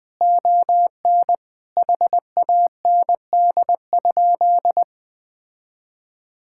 Morse code, heard as a keyed tone, about 20 wpm, 700 Hz. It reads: ON HAND?
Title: Sentences from Top 1000 Words - Review Part 5 - 20wpm